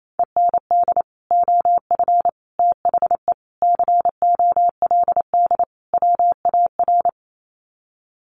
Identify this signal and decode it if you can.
Morse code, heard as a keyed tone, about 28 wpm, 700 Hz. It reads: END OF THE COLD WAR
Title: Sets of 5 Words Encore 28wpm